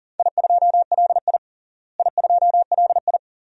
Morse code, heard as a keyed tone, about 38 wpm, 700 Hz. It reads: I2LI I2LI
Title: DXCC-100 Call Signs - 4 Characters - Repeated 2x - 40wpm